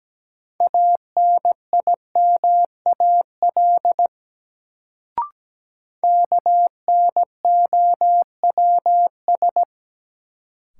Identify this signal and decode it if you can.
Morse code, heard as a keyed tone, about 17 wpm, 700 Hz. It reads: ANIMAL KNOWS
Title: Sentences from Top 1000 Words - Review Part 1 - New Word Drill - 17wpm